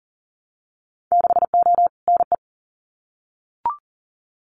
Morse code, heard as a keyed tone, about 40 wpm, 700 Hz. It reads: 6O NE
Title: Winter Field Day Exchanges Encore 40wpm